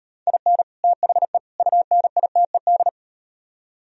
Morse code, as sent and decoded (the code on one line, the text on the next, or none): .. -. / - .... / ..- -. .. - . -..
IN THE UNITED